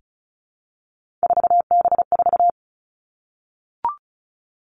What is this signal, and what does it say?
Morse code, heard as a keyed tone, about 36 wpm, 700 Hz. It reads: 4B4
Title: Three Challenging Letter-Number Combinations 35wpm